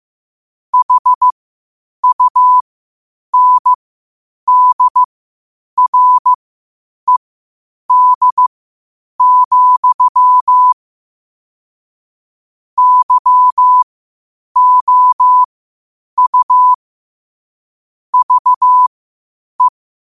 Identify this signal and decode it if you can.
Morse code, heard as a keyed tone, about 15 wpm, spaced as for 8 wpm, 1000 Hz. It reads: HUNDRED, YOU VE
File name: qotdfiles / 8wpm